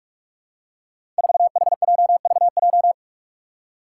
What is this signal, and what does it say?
Morse code, heard as a keyed tone, about 44 wpm, 700 Hz. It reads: 4HJVJ
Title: Five Challenging Letter-Number Combinations 45wpm